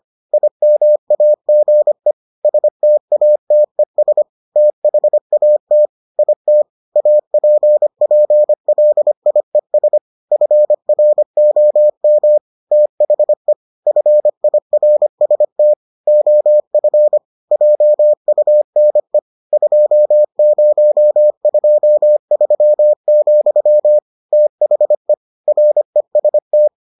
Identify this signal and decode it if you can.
Morse code, heard as a keyed tone, about 25 wpm, 600 Hz. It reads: IMAGE STATES THAT IT APPLIES FROM THE FIRST OF JUNE 2023, THE REST